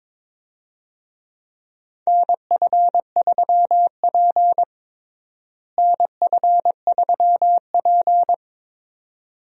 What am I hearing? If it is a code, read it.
Morse code, heard as a keyed tone, about 22 wpm, 700 Hz. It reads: NF3P NF3P
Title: USA Call Signs - 4 Characters - Repeated 2x - 22wpm